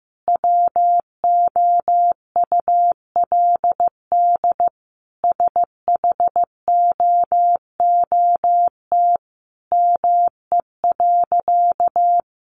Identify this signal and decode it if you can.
Morse code, heard as a keyed tone, about 15 wpm, 700 Hz. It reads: WOULD SHOOT ME.
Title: Sentences from Top 1500 Words 15wpm